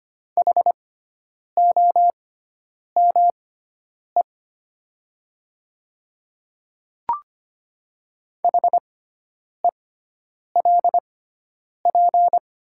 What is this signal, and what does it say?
Morse code, heard as a keyed tone, about 25 wpm, spaced as for 9 wpm, 700 Hz. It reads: HOME HELP
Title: 6x Character Spacing - Top 200 Words - 25wpm